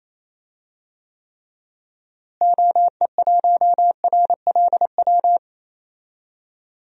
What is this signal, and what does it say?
Morse code, heard as a keyed tone, about 28 wpm, 700 Hz. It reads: OE1RLW
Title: DXCC-100 Call Signs - 6 Characters - Repeated 3x - 28wpm